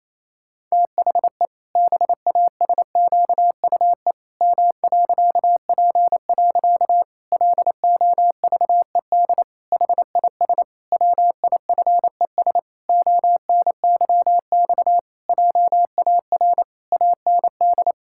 Morse code, heard as a keyed tone, about 28 wpm, 700 Hz. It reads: THE BASQUE M.P. LOVED HIS WIFES ONYX JAR AND
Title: Pangram Challenge v2 - 28wpm